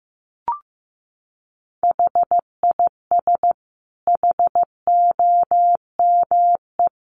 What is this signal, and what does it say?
Morse code, heard as a keyed tone, about 15 wpm, 700 Hz. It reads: HIS HOME
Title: Sentences from Top 1500 Words 15wpm